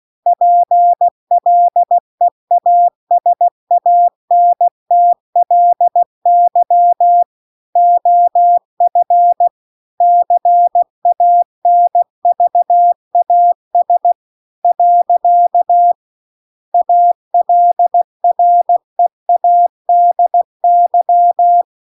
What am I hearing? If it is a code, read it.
Morse code, heard as a keyed tone, about 16 wpm, 700 Hz. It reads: PLEASANTLY OF CANVAS . ALREADY